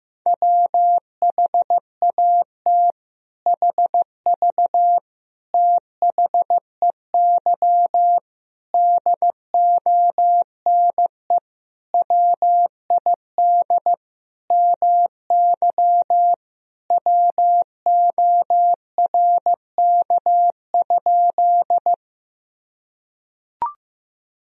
Morse code, heard as a keyed tone, about 15 wpm, 700 Hz. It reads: WHAT HV THEY DONE WID MY WORK?
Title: Sentences from Top 500 Words and Abbreviations 15wpm